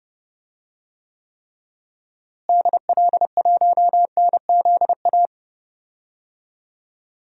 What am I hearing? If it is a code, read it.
Morse code, heard as a keyed tone, about 30 wpm, 700 Hz. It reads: DL1NZA
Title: DXCC-100 Call Signs - 6 Characters - Repeated 3x - 30wpm